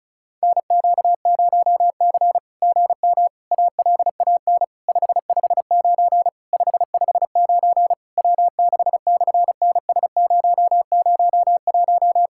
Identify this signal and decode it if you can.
Morse code, heard as a keyed tone, about 33 wpm, 700 Hz. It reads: NQ0C GM ALAN 559 559 W6/NS001
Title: SOTA QSO Practice 35wpm